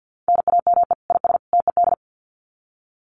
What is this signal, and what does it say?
Morse code, heard as a keyed tone, about 50 wpm, 700 Hz. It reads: NAME IS TED
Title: Common QSO Phrases 50wpm